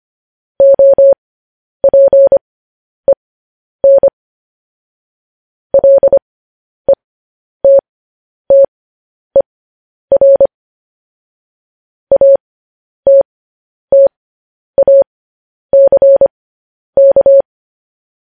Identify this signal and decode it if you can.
Morse code, heard as a keyed tone, about 25 wpm, spaced as for 10 wpm, 550 Hz. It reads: OPEN LETTER ATTACK
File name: podcast / News headlines-1x-25x10